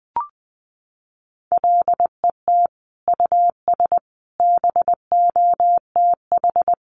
Morse code, heard as a keyed tone, about 20 wpm, 700 Hz. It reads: LET US BOTH
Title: Sentences from Top 1200 Words 20wpm